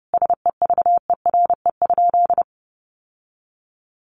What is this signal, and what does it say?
Morse code, heard as a keyed tone, about 30 wpm, 700 Hz. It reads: SEVERE?
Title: Sentences from Top 1400 Words 30wpm